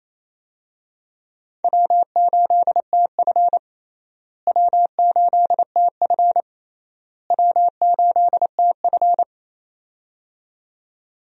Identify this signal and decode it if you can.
Morse code, heard as a keyed tone, about 28 wpm, 700 Hz. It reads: W8TF W8TF W8TF
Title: USA Call Signs - 4 Characters - Repeated 3x - 28wpm